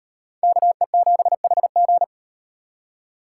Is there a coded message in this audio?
Yes: Morse code from 0.4 to 2.0 s, heard as a keyed tone, about 36 wpm, 700 Hz. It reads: KE7HG